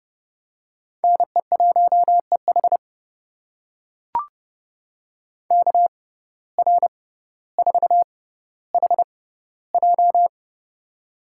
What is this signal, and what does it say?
Morse code, heard as a keyed tone, about 30 wpm, spaced as for 15 wpm, 700 Hz. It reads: NE1EH KR4HJ
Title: USA Call Signs - 5 Characters - 6x Character Spacing - 30wpm